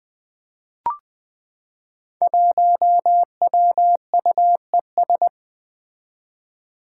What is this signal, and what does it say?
Morse code, heard as a keyed tone, about 20 wpm, 700 Hz. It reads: 1WUES
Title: Five Challenging Letter-Number Combinations 20wpm